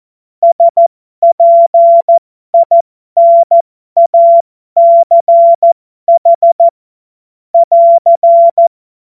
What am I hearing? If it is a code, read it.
Morse code, heard as a keyed tone, about 14 wpm, spaced as for 12 wpm, 680 Hz. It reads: SPINACH <AR>
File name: Morsekurs / Witze